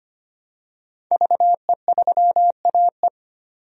Morse code, heard as a keyed tone, about 25 wpm, 700 Hz. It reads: VE3AE
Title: DXCC-100 Call Signs - 5 Characters - 25wpm